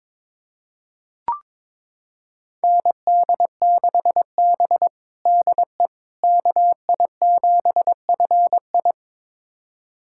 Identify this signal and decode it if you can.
Morse code, heard as a keyed tone, about 21 wpm, 700 Hz. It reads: ND6B DE KI7FI
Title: USA Callsign DE Callsign 22wpm